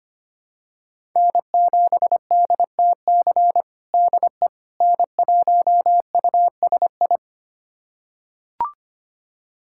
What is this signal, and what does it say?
Morse code, heard as a keyed tone, about 25 wpm, 700 Hz. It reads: N7DTC DE N1USI
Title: USA Callsign DE Callsign - 5 Characters - 25wpm